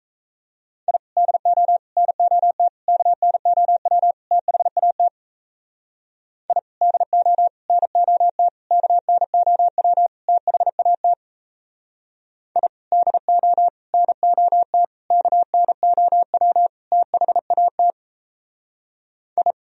Morse code, heard as a keyed tone, about 38 wpm, 700 Hz. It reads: I DO NOT KNOW THAT I DO NOT KNOW THAT I DO NOT KNOW THAT I